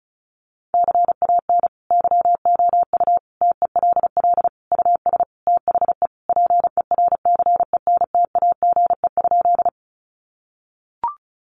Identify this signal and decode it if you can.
Morse code, heard as a keyed tone, about 34 wpm, 700 Hz. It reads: CAN YOU TELL US THE PERCENTAGE?
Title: Sentences from Top 1300 Words 35wpm